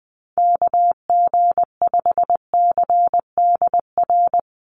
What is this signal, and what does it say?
Morse code, heard as a keyed tone, about 20 wpm, 700 Hz. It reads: KG5CDR
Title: USA Call Signs 20wpm